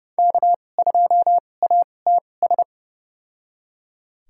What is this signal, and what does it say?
Morse code, heard as a keyed tone, about 30 wpm, spaced as for 22 wpm, 700 Hz. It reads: K2ATS
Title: USA Call Signs - 5 Characters - 2x Character Spacing - 30wpm